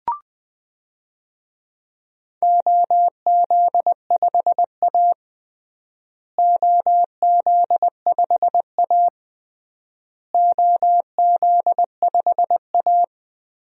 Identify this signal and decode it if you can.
Morse code, heard as a keyed tone, about 20 wpm, 700 Hz. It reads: OZ5A OZ5A OZ5A
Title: DXCC-100 Call Signs - 4 Characters - Repeated 3x - 20wpm